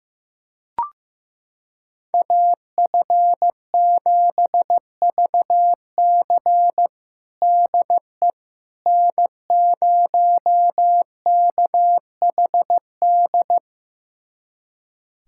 Morse code, heard as a keyed tone, about 16 wpm, 700 Hz. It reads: AF7VC DE N0KHD
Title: USA Callsign DE Callsign - 5 Characters - 15wpm